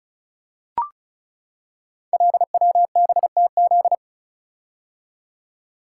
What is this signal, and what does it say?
Morse code, heard as a keyed tone, about 36 wpm, 700 Hz. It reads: LWBTZ
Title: Code Groups - 5 Letter:Numbers - 35wpm